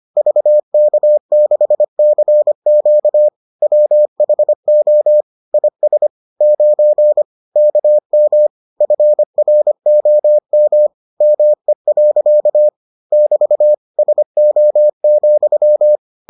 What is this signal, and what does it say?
Morse code, heard as a keyed tone, about 25 wpm, 600 Hz. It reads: VK6CQ WHO IS 9 KM FROM ME. <BT> SO,